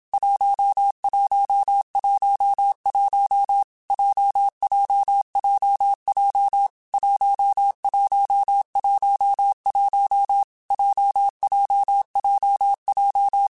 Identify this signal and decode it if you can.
Morse code, heard as a keyed tone, about 27 wpm, 800 Hz. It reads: 1111 JJJJ 1111 JJJJ
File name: Groups@25 / Groups@25 mp3s / JZ17@25